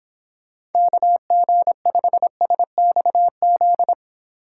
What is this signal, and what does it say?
Morse code, heard as a keyed tone, about 30 wpm, 700 Hz. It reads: KG5SXZ